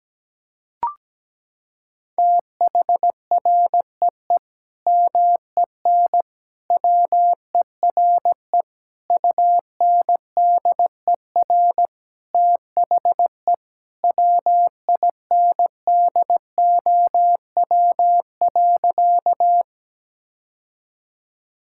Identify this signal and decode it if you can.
Morse code, heard as a keyed tone, about 17 wpm, 700 Hz. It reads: THREE MEN WERE UNDER THE WINDOW.
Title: Sentences from Top 1600 Words 17wpm